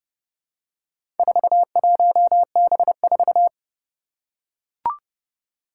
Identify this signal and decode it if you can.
Morse code, heard as a keyed tone, about 30 wpm, 700 Hz. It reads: 41B4
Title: Four Challenging Letter-Number Combinations 30wpm